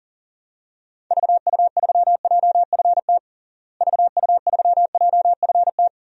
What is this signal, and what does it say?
Morse code, heard as a keyed tone, about 37 wpm, 700 Hz. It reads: VU3JFT VU3JFT
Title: DXCC-100 Call Signs - 6 Characters - Repeated 3x - 40wpm